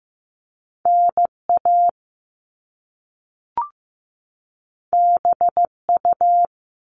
Morse code, heard as a keyed tone, about 15 wpm, 700 Hz. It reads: NA BU
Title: Top Two Letter Combinations in Words 15wpm